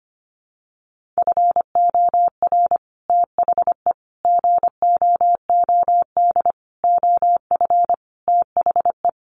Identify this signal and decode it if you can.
Morse code, heard as a keyed tone, about 25 wpm, 700 Hz. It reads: FOR THE GOOD OF THE
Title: Sets of 5 Words Encore2 25wpm